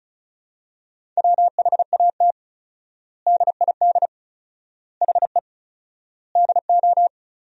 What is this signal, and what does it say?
Morse code, heard as a keyed tone, about 36 wpm, 700 Hz. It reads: WHAT DID HE DO
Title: 4x Word-Spacing - Sentences from Top 100 Words - 35wpm